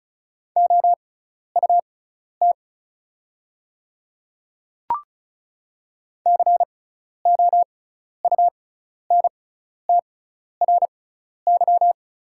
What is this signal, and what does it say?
Morse code, heard as a keyed tone, about 36 wpm, spaced as for 12 wpm, 690 Hz. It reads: OUT COUNTRY
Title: 6x Character Spacing - Top 300 Words - 35wpm